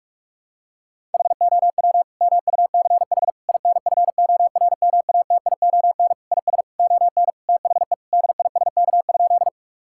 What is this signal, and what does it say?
Morse code, heard as a keyed tone, about 47 wpm, 700 Hz. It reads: HOW MUCH INFORMATION IS ON THE DISK?
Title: Sentences from Top 1000 Words - Review Part 6 - 45wpm